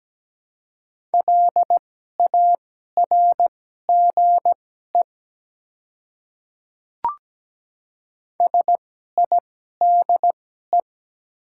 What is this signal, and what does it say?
Morse code, heard as a keyed tone, about 17 wpm, spaced as for 12 wpm, 700 Hz. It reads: LARGE SIDE
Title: 2x Character Spacing - Top 200 Words - 17wpm